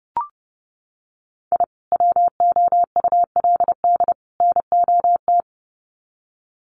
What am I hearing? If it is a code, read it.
Morse code, heard as a keyed tone, about 30 wpm, 700 Hz. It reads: I WOULD NOT
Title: Sets of 3 Words 30wpm